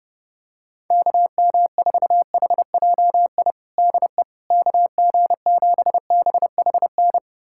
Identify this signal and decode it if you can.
Morse code, heard as a keyed tone, about 30 wpm, 700 Hz. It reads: KM4HJI DE KG7BHN
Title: USA Callsign DE Callsign - 6 Characters - 30wpm